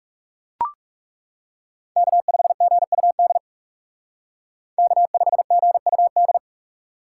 Morse code, heard as a keyed tone, about 38 wpm, 700 Hz. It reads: K5GUD K5GUD